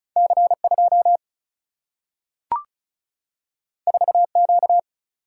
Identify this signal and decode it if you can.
Morse code, heard as a keyed tone, about 36 wpm, 700 Hz. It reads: C2 4Q
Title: Single-Letter Single-Number 35wpm